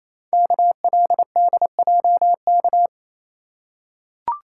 Morse code, heard as a keyed tone, about 27 wpm, 700 Hz. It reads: KLDJK
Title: Code Groups - 5 Letters - 28wpm